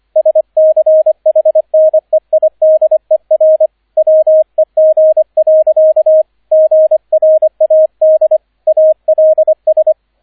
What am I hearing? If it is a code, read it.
Morse code, heard as a keyed tone, about 25 wpm, 630 Hz, with radio noise. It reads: SCHNEIDER WEG. GRAD ALS
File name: 003 Dritter Streich 25wpm